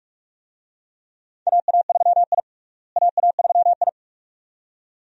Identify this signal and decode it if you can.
Morse code, heard as a keyed tone, about 42 wpm, 700 Hz. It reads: AA3I AA3I
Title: USA Call Signs - 4 Characters - Repeated 2x - 45wpm